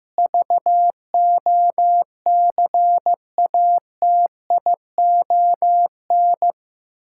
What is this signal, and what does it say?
Morse code, heard as a keyed tone, about 15 wpm, 700 Hz. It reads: VOCATION
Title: Pangram Challenge v2 - 15wpm